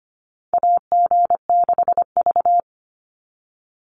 Morse code, heard as a keyed tone, about 25 wpm, 700 Hz. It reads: AG6V
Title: NA SOTA Chaser Callsigns 25wpm